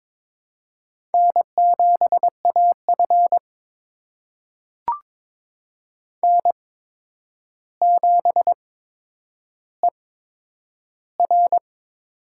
Morse code, heard as a keyed tone, about 22 wpm, spaced as for 9 wpm, 700 Hz. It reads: N7AF N7ER
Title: USA Call Signs - 4 Characters - 8x Character Spacing - 22wpm